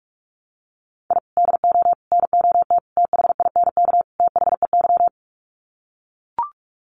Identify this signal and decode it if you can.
Morse code, heard as a keyed tone, about 44 wpm, 700 Hz. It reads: I DO NOT THINK THEY
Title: Sets of 5 Words 45wpm